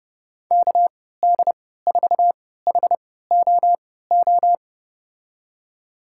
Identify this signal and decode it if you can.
Morse code, heard as a keyed tone, about 30 wpm, spaced as for 17 wpm, 700 Hz. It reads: KD4HOO
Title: USA Call Signs - 6 Characters - 3x Character Spacing - 30wpm